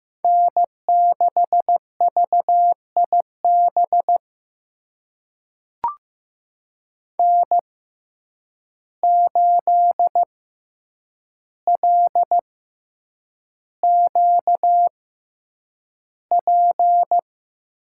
Morse code, heard as a keyed tone, about 15 wpm, spaced as for 8 wpm, 700 Hz. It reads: N6VIB N8LQP
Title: USA Call Signs - 5 Characters - 6x Character Spacing - 15wpm